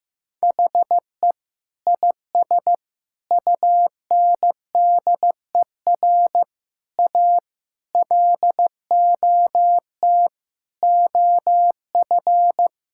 Morse code, heard as a keyed tone, about 15 wpm, 700 Hz. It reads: HE IS UNDER A LOT OF
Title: Sentences from Top 1000 Words - Review Part 6 - 15wpm